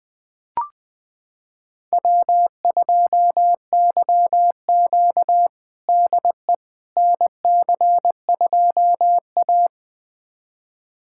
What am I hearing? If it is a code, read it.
Morse code, heard as a keyed tone, about 21 wpm, 700 Hz. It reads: W2YQ DE NC2A
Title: USA Callsign DE Callsign - 4 Characters - 20wpm